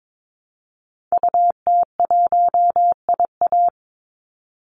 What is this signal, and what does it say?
Morse code, heard as a keyed tone, about 22 wpm, 700 Hz. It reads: UT1IA
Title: Call Signs - 22wpm